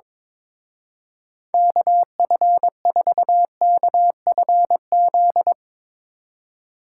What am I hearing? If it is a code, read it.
Morse code, heard as a keyed tone, about 22 wpm, 700 Hz. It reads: KF4KFZ
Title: Speed Racing - USA 2x3 Call Signs 22wpm